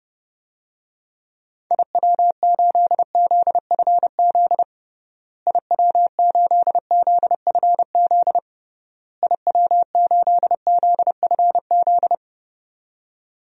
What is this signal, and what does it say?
Morse code, heard as a keyed tone, about 30 wpm, 700 Hz. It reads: IW8ZFZ IW8ZFZ IW8ZFZ
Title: DXCC-100 Call Signs - 6 Characters - Repeated 3x - 30wpm